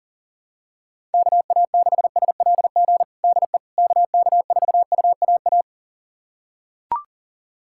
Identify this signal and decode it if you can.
Morse code, heard as a keyed tone, about 40 wpm, 700 Hz. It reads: KA6SLG DE KK4UAA
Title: USA Callsign DE Callsign - 6 Characters - 40wpm